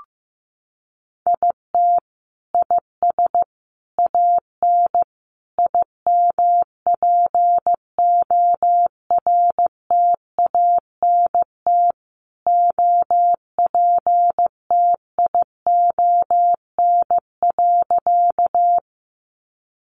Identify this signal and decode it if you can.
Morse code, heard as a keyed tone, about 15 wpm, 700 Hz. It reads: IT IS AN IMPORTANT OPTION.
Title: Sentences from Top 1300 Words 15wpm